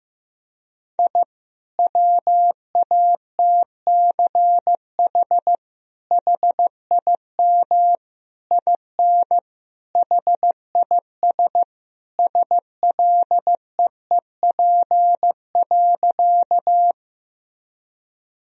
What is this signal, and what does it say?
Morse code, heard as a keyed tone, about 15 wpm, 700 Hz. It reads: I WATCH HIM IN HIS SLEEP.
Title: Sentences from Top 1000 Words - Review Part 1 - 15wpm